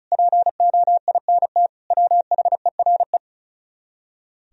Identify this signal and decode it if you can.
Morse code, heard as a keyed tone, about 46 wpm, 700 Hz. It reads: POINT WHERE